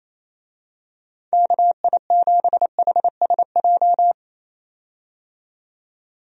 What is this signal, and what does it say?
Morse code, heard as a keyed tone, about 28 wpm, 700 Hz. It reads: KI7HSJ
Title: USA Call Signs - 6 Characters - 28wpm